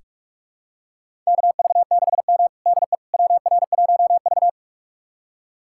Morse code, heard as a keyed tone, about 46 wpm, 700 Hz. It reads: KV6M DE WR1V
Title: USA Callsign DE Callsign - 4 Characters - 45wpm